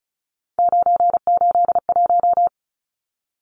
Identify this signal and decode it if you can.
Morse code, heard as a keyed tone, about 34 wpm, 700 Hz. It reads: 981
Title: Numbers - 3-Digit 35wpm